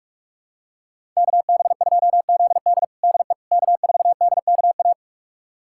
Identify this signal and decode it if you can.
Morse code, heard as a keyed tone, about 42 wpm, 700 Hz. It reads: KB2ZD DE K4DKA
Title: USA Callsign DE Callsign - 5 Characters - 45wpm